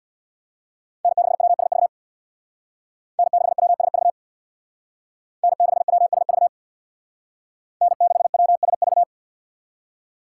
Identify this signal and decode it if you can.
Morse code, heard as a keyed tone, about 62 wpm, 700 Hz. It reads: N6WSV N6WSV N6WSV N6WSV
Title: Speed Racing - USA 1x3 Call Signs 50wpm